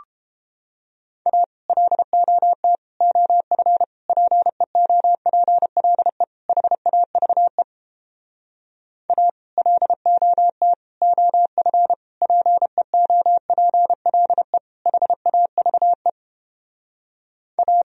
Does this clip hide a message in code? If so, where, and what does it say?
Yes: Morse code from 1.3 to 17.8 s, heard as a keyed tone, about 31 wpm, 700 Hz. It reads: A LOT OF PEOPLE HAVE A LOT OF PEOPLE HAVE A